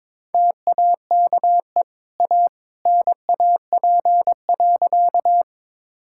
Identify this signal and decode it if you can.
Morse code, heard as a keyed tone, about 22 wpm, 690 Hz. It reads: TAKE A NAP.
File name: practice / 22wpm / Easy Sentences with Hints - 22wpm